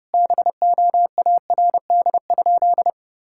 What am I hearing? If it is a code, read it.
Morse code, heard as a keyed tone, about 32 wpm, 700 Hz. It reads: BOARD?
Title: Sentences from Top 1000 Words - Review Part 6 - 30wpm